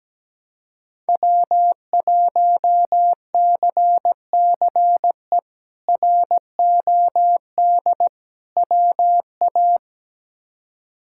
Morse code, heard as a keyed tone, about 17 wpm, 700 Hz. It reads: W1CCE ROD WA
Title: K1USN SST QSO Practice - 17wpm